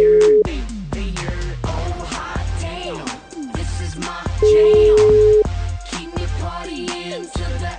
Ringback tone (Europe, 425 Hz).